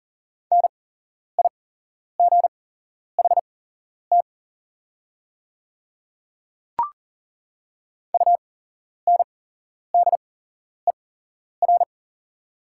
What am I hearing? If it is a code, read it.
Morse code, heard as a keyed tone, about 40 wpm, spaced as for 11 wpm, 690 Hz. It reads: NIGHT UNDER
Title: 8x Character Spacing - Top 500 Words - 40wpm